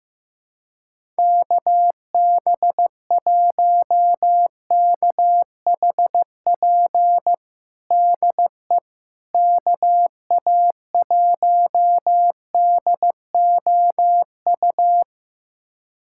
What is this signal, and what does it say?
Morse code, heard as a keyed tone, about 15 wpm, 700 Hz. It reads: KB1KHP DE KA1DOU
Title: USA Callsign DE Callsign 15wpm